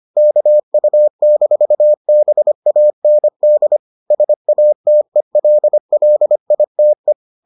Morse code, heard as a keyed tone, about 25 wpm, 600 Hz. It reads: KU-BAND SATELLITE